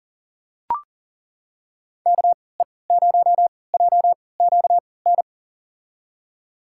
Morse code, heard as a keyed tone, about 39 wpm, spaced as for 22 wpm, 700 Hz. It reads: KE0JQN
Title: USA Call Signs - 6 Characters - 3x Character Spacing - 40wpm